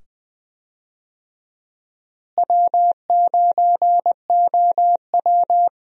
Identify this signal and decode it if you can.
Morse code, heard as a keyed tone, about 20 wpm, 700 Hz. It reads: W9OW
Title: USA Call Signs - 4 Characters - Repeated 2x - 20wpm